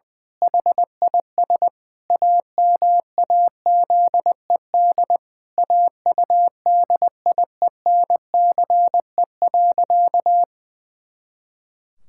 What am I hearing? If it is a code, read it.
Morse code, heard as a keyed tone, about 21 wpm, 700 Hz. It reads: HIS AMAZED AUDIENCE.
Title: Pangram Challenge v2 - 20wpm